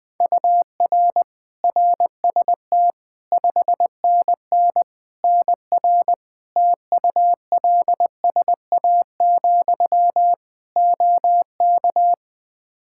Morse code, heard as a keyed tone, about 20 wpm, 700 Hz. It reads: UR RST 5NN NR TULSA, OK
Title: Practice QSO - Exchange 1 - Beginning - 20wpm